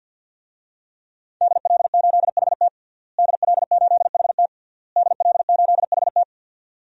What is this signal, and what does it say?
Morse code, heard as a keyed tone, about 53 wpm, 700 Hz. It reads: DL8HT DL8HT DL8HT
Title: DXCC-100 Call Signs - 5 Characters - Repeated 3x - 50wpm